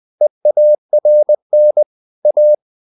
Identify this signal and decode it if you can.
Morse code, heard as a keyed tone, about 20 wpm, 600 Hz. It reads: EARN A
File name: CW / Practice / EnglishQuotations0320